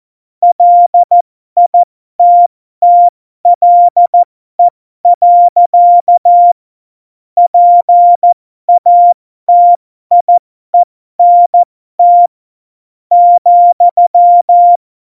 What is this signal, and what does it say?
Morse code, heard as a keyed tone, about 14 wpm, spaced as for 11 wpm, 700 Hz. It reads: LITTLE. PATIENT,
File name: Morsekurs / Witze